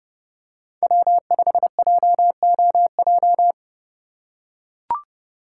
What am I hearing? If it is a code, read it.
Morse code, heard as a keyed tone, about 31 wpm, 700 Hz. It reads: W5JOJ